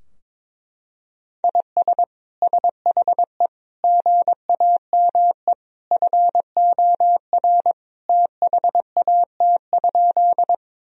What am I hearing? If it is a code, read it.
Morse code, heard as a keyed tone, about 22 wpm, 700 Hz. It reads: IS SHE GAME FOR THAT?
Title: Sentences from Top 1000 Words - Review Part 3 - 22wpm